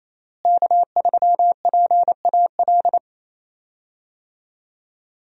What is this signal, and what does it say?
Morse code, heard as a keyed tone, about 28 wpm, 700 Hz. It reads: K3PAL
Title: USA Call Signs - 5 Characters - Repeated 2x - 28wpm